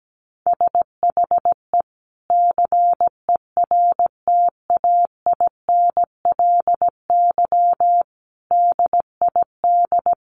Morse code, heard as a keyed tone, about 17 wpm, 700 Hz. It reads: SHE CERTAINLY DID
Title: Sentences from Top 1100 Words 17wpm